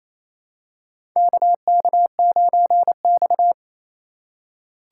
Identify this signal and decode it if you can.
Morse code, heard as a keyed tone, about 28 wpm, 700 Hz. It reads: KK9X
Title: USA Call Signs - 4 Characters - 2x Character Spacing - 28wpm